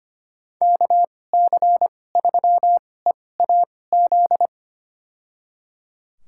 Morse code, heard as a keyed tone, about 25 wpm, spaced as for 18 wpm, 700 Hz. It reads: KC3EAZ
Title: USA Call Signs - 6 Characters - 2x Character Spacing - 25wpm